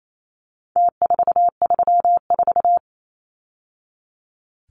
Morse code, heard as a keyed tone, about 28 wpm, 700 Hz. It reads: T434
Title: Numbers - Contest Style - 4-Digit 28wpm